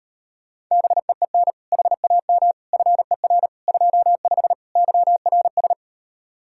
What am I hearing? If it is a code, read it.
Morse code, heard as a keyed tone, about 39 wpm, 700 Hz. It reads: BEEN HAM FER 25 YRS